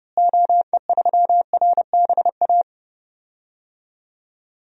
Morse code, heard as a keyed tone, about 30 wpm, 700 Hz. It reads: OE3RBA
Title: DXCC-100 Call Signs - 6 Characters - Repeated 3x - 30wpm